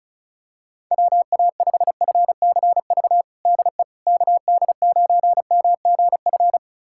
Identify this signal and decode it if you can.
Morse code, heard as a keyed tone, about 36 wpm, 700 Hz. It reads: WA5FCV DE KD9MGF
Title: USA Callsign DE Callsign - 6 Characters - 35wpm